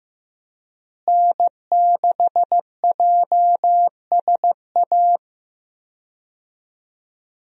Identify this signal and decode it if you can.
Morse code, heard as a keyed tone, about 16 wpm, 700 Hz. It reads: N6JSA
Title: USA Call Signs - 5 Characters - Repeated 3x - 15wpm